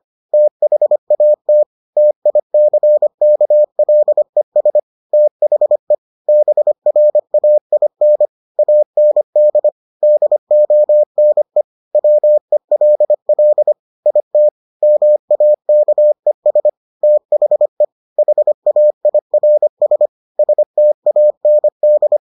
Morse code, heard as a keyed tone, about 25 wpm, 600 Hz. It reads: THAT TICKLES THE BRAIN AND DONE WELL IT MAKES THE HAIRS STAND